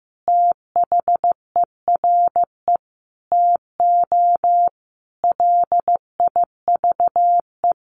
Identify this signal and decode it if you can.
Morse code, heard as a keyed tone, about 15 wpm, 700 Hz. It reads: THERE TO LIVE
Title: Sentences from Top 200 Words 15wpm